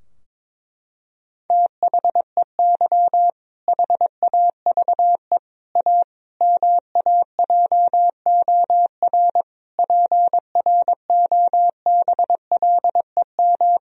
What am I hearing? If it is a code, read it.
Morse code, heard as a keyed tone, about 22 wpm, 700 Hz. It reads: THEY HAVE A MAJOR PROBLEM
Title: Sentences from Top 1000 Words - Review Part 4 - 22wpm